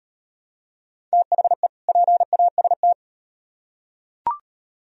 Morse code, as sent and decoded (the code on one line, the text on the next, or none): - .... / .--. .- ... -
THE PAST